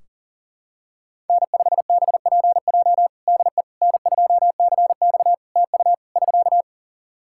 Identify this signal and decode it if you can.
Morse code, heard as a keyed tone, about 37 wpm, 700 Hz. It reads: N5BPJ DE N2CX TU <SK>
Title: K1USN SST QSO Practice 40wpm